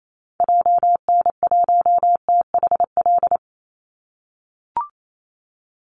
Morse code, heard as a keyed tone, about 28 wpm, 700 Hz. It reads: JN1THL
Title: Call Signs - 28wpm